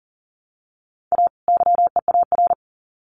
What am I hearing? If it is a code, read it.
Morse code, heard as a keyed tone, about 40 wpm, 700 Hz. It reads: A YEAR
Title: Sets of 2 Words 40wpm